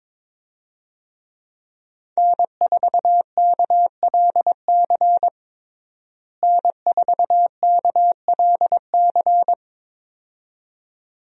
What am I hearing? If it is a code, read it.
Morse code, heard as a keyed tone, about 22 wpm, 700 Hz. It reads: N4KLC N4KLC
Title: USA Call Signs - 5 Characters - Repeated 2x - 22wpm